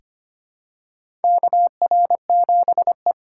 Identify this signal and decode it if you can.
Morse code, heard as a keyed tone, about 25 wpm, 700 Hz. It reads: KR7E
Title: USA Call Signs - 4 Characters - 3x Character Spacing - 25wpm